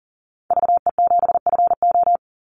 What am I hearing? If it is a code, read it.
Morse code, heard as a keyed tone, about 40 wpm, 700 Hz. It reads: VE7FO
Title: Call Signs Repeated 3x - 40wpm